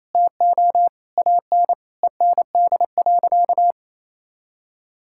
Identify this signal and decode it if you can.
Morse code, heard as a keyed tone, about 27 wpm, 700 Hz. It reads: TO AN END.
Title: Sentences from Top 1000 Words - Review Part 5 - 28wpm